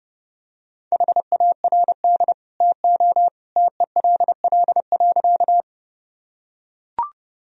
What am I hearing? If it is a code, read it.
Morse code, heard as a keyed tone, about 30 wpm, 700 Hz. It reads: HARD TO TELL.
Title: Sentences from Top 1000 Words - Review Part 1 - 30wpm